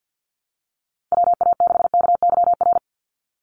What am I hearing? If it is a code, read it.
Morse code, heard as a keyed tone, about 50 wpm, 700 Hz. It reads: WA6KYR
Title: NA SOTA Chaser Callsigns 50wpm